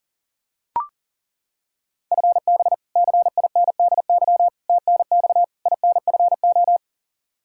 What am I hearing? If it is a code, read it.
Morse code, heard as a keyed tone, about 40 wpm, 700 Hz. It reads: FB CINDY TNX INFO
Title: Practice QSO - Exchange 3 - More Info - 40wpm